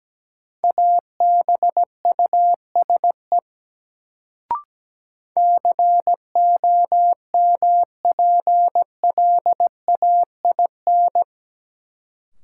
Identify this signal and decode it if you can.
Morse code, heard as a keyed tone, about 17 wpm, 700 Hz. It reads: ABUSE COMPLAIN
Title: Sentences from Top 1300 Words - New Word Drill Alt - 17wpm